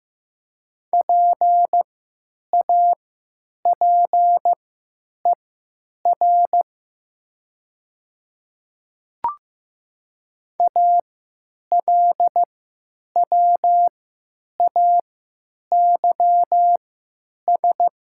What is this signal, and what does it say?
Morse code, heard as a keyed tone, about 15 wpm, spaced as for 9 wpm, 700 Hz. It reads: PAPER ALWAYS
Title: 3x Character Spacing - Top 300 Words - 15wpm